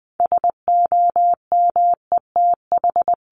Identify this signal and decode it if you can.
Morse code, heard as a keyed tone, about 20 wpm, 700 Hz. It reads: SOMETH
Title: Sentences from Top 1500 Words 20wpm